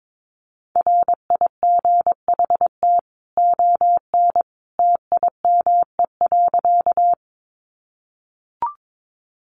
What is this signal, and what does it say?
Morse code, heard as a keyed tone, about 22 wpm, 700 Hz. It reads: RIGHT ON TIME.
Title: A Sound Of Thunder 22wpm